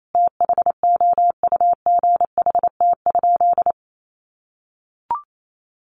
Morse code, heard as a keyed tone, about 28 wpm, 700 Hz. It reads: THOUGHT?
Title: Sentences from Top 300 Words 28wpm